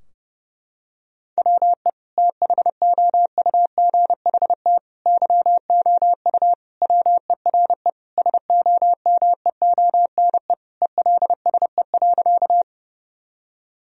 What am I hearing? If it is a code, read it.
Morse code, heard as a keyed tone, about 32 wpm, 700 Hz. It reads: WE THOUGHT YOU WERE SOMEONE ELSE.
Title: Sentences from Top 1000 Words - Review Part 4 - 30wpm